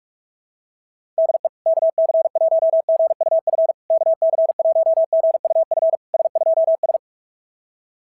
Morse code, heard as a keyed tone, about 44 wpm, 660 Hz. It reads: DE KC1GUF KC1GUF S2S
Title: SOTA QSO Practice 45wpm